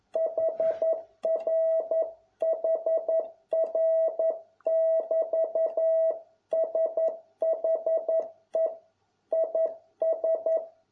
Morse code, heard as a keyed tone, about 11 wpm, 650 Hz. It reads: HRHR<BT>SHE IS